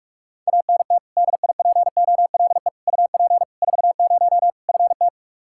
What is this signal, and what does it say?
Morse code, heard as a keyed tone, about 46 wpm, 700 Hz. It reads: ANT DIPOLE UP 40 FT
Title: Practice QSO - Exchange 2 - Info - 45wpm